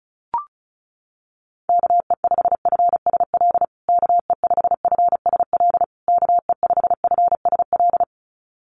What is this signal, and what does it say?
Morse code, heard as a keyed tone, about 34 wpm, 700 Hz. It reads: KE5FSL KE5FSL KE5FSL
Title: USA Call Signs Repeated 3x 35wpm